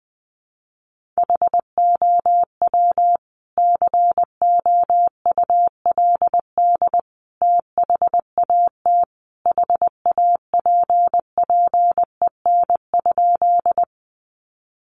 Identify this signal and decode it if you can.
Morse code, heard as a keyed tone, about 20 wpm, 700 Hz. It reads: HOW COULD THAT HAPPEN?
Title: Sentences from Top 400 Words 20wpm